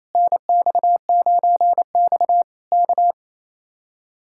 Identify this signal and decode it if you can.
Morse code, heard as a keyed tone, about 26 wpm, 700 Hz. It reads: NX9X K
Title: Speed-Racing Common QSO Phrases Encore - 28wpm